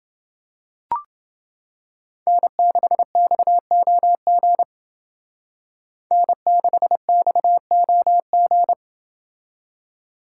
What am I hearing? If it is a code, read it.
Morse code, heard as a keyed tone, about 28 wpm, 700 Hz. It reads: N6XOG N6XOG